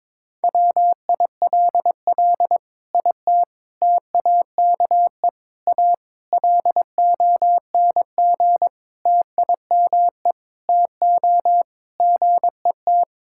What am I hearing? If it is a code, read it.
Morse code, heard as a keyed tone, about 22 wpm, 700 Hz. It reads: WILL IT TAKE A LONG TIME TO GET
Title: Sentences from Top 1000 Words - Review Part 6 - 22wpm